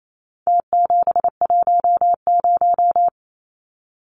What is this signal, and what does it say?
Morse code, heard as a keyed tone, about 28 wpm, 700 Hz. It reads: T710
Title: Numbers - Contest Style - 4-Digit 28wpm